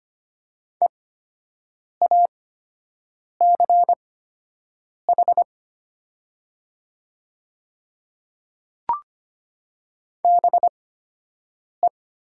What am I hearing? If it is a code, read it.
Morse code, heard as a keyed tone, about 25 wpm, spaced as for 7 wpm, 700 Hz. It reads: EACH BE
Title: 8x Character Spacing - Top 100 Words - 25wpm